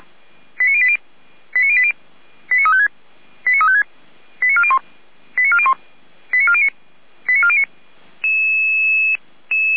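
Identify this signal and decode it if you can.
Five-tone selective call (ZVEI), heard as quick runs of five tones, tones of about 70 ms. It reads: 89080 89080 89356 89356 89401 89401 89409 89409
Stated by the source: ZVEI five-tone selective call received on SDR.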